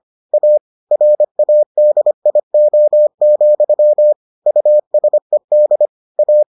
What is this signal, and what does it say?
Morse code, heard as a keyed tone, about 25 wpm, 600 Hz. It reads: A RADIO, USED A